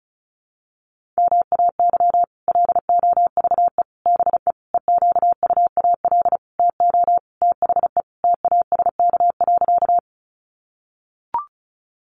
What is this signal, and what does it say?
Morse code, heard as a keyed tone, about 34 wpm, 700 Hz. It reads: MAY LOVE BE EQUAL TO THE TASK.
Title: Sentences from Top 1400 Words 35wpm